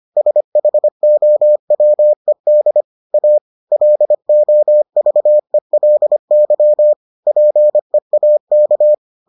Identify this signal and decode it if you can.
Morse code, heard as a keyed tone, about 25 wpm, 600 Hz. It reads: SHOWED A LOVELY PEAK